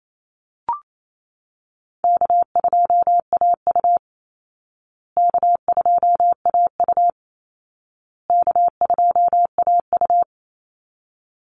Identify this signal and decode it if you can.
Morse code, heard as a keyed tone, about 28 wpm, 700 Hz. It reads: K2AU K2AU K2AU
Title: Easy Call Signs Repeated 3x - 28wpm